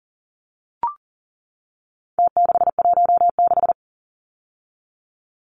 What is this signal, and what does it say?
Morse code, heard as a keyed tone, about 40 wpm, 700 Hz. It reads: T616
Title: Numbers - Contest Style - 4-Digit 40wpm